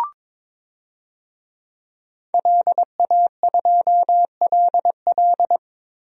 Morse code, heard as a keyed tone, about 22 wpm, 700 Hz. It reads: LA2LL
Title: DXCC-100 Call Signs - 5 Characters - Repeated 2x - 22wpm